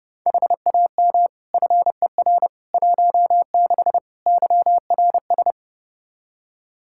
Morse code, heard as a keyed tone, about 30 wpm, 700 Hz. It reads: HAM FER 16 YRS
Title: Practice QSO - Exchange 3 - More Info - 30wpm